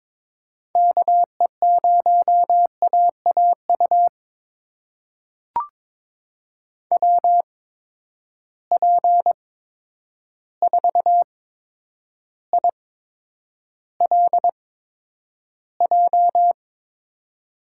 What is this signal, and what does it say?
Morse code, heard as a keyed tone, about 22 wpm, spaced as for 9 wpm, 700 Hz. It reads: KE0AAU WP4ILJ
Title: USA Call Signs - 6 Characters - 8x Character Spacing - 22wpm